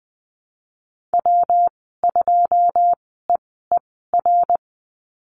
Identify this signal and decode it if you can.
Morse code, heard as a keyed tone, about 20 wpm, spaced as for 15 wpm, 700 Hz. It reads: W2EER
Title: 167 Lesson 27g - Number 7 - Call Signs 20wpm